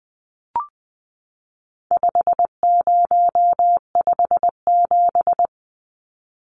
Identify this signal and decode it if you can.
Morse code, heard as a keyed tone, about 20 wpm, 700 Hz. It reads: 5057
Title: Numbers - 4-Digit 20wpm